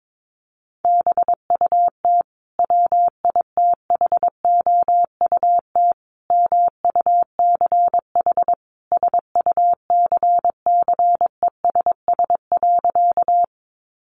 Morse code, heard as a keyed tone, about 22 wpm, 700 Hz. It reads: BUT WITHOUT MUCH SUCCESS.
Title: Sentences from Top 900 Words 22wpm